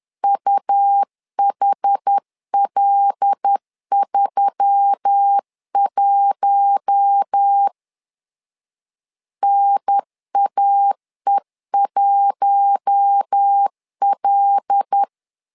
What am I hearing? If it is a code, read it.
Morse code, heard as a keyed tone, about 10 wpm, 800 Hz. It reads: UHL31 NAE1L
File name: LEZIONI / 5 LBVC / 5 L 10wpm 2259